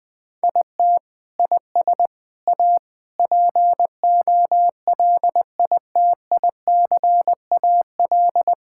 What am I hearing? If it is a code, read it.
Morse code, heard as a keyed tone, about 20 wpm, 700 Hz. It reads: IT IS A POLITICAL
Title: Sentences from Top 1000 Words - Review Part 5 - 20wpm